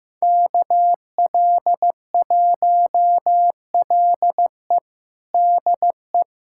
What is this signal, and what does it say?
Morse code, heard as a keyed tone, about 16 wpm, 700 Hz. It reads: KL1LE DE
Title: USA Callsign DE Callsign - 5 Characters - 15wpm